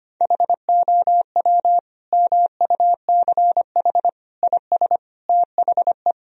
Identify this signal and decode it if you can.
Morse code, heard as a keyed tone, about 25 wpm, 700 Hz. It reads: HOW MUCH IS THE
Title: Sentences from Top 1000 Words - Review Part 4 - 25wpm